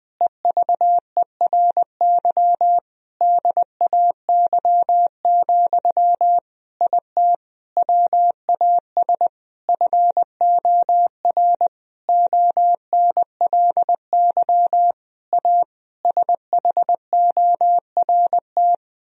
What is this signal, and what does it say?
Morse code, heard as a keyed tone, about 19 wpm, 700 Hz. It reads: EVERY DAY, IT WAS FOR ONLY A SHORT
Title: Sentences from Top 1000 Words - Review Part 3 - 20wpm